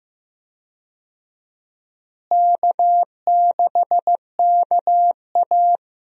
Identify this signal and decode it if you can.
Morse code, heard as a keyed tone, about 15 wpm, 700 Hz. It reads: K6KA
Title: USA Call Signs - 4 Characters - Repeated 2x - 15wpm